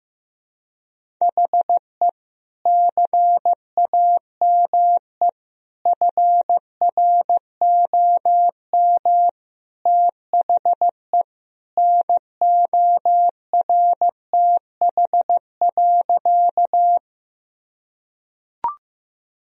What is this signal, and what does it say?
Morse code, heard as a keyed tone, about 15 wpm, 700 Hz. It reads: HE CAME FROM THE NORTH.
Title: Sentences from Top 1000 Words - Review Part 1 - 15wpm